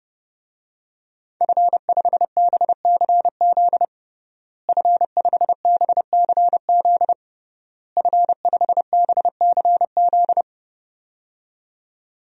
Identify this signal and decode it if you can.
Morse code, heard as a keyed tone, about 30 wpm, 700 Hz. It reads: F5BCZ F5BCZ F5BCZ
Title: DXCC-100 Call Signs - 5 Characters - Repeated 3x - 30wpm